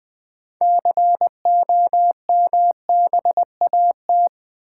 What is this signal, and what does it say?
Morse code, heard as a keyed tone, about 20 wpm, 700 Hz. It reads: COMBAT
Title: Build A Word - 3 Letters At A Time - 6-Letter Words - 20wpm